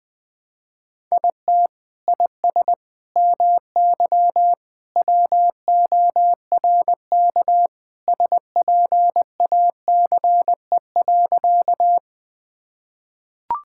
Morse code, heard as a keyed tone, about 20 wpm, 700 Hz. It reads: IT IS MY WORK SPACE.
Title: Sentences from Top 1000 Words - Review Part 5 - 20wpm